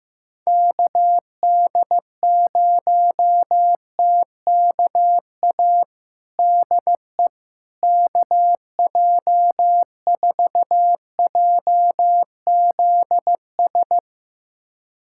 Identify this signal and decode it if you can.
Morse code, heard as a keyed tone, about 15 wpm, 700 Hz. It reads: KD0TKA DE KJ4JZS
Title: USA Callsign DE Callsign - 6 Characters - 15wpm